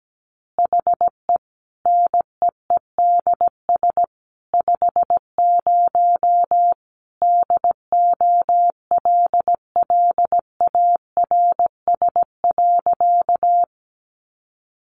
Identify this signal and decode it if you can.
Morse code, heard as a keyed tone, about 17 wpm, 700 Hz. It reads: HE NEEDS 50 DOLLARS.
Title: Additional Easy Phrases 17wpm